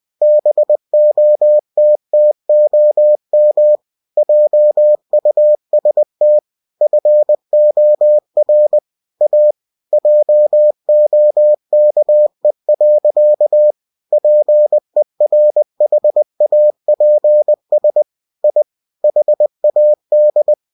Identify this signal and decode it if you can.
Morse code, heard as a keyed tone, about 20 wpm, 600 Hz. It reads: BOTTOM JUST FOR A JOKE. PERHAPS I HAD
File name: cw-books / winnie-the-pooh / winnie-20wpm-page-0037